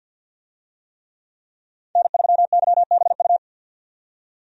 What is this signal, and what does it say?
Morse code, heard as a keyed tone, about 49 wpm, 700 Hz. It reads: N3YBU
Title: USA Call Signs - 5 Characters - 50wpm